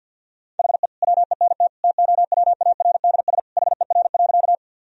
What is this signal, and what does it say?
Morse code, heard as a keyed tone, about 50 wpm, 700 Hz. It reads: HE WENT TOWARDS HER.
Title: Sentences from Top 1000 Words - Review Part 2 - 50wpm